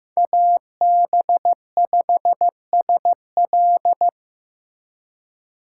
Morse code, heard as a keyed tone, about 15 wpm, 700 Hz. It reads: AB5SL
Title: USA Call Signs - 5 Characters - 4x Character Spacing - 15wpm